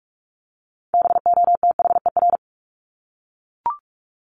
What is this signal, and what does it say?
Morse code, heard as a keyed tone, about 44 wpm, 700 Hz. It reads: BOTHER
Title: Top 1501-1750 Words 45wpm